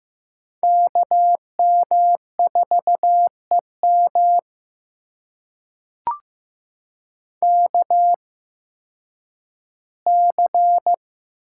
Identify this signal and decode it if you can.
Morse code, heard as a keyed tone, about 15 wpm, spaced as for 10 wpm, 700 Hz. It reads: KM4EM KC
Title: USA Call Signs - 8x Character Spacing - 15wpm